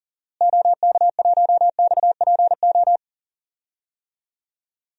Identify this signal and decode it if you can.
Morse code, heard as a keyed tone, about 39 wpm, 700 Hz. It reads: OK1XPO
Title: DXCC-100 Call Signs - 6 Characters - Repeated 2x - 40wpm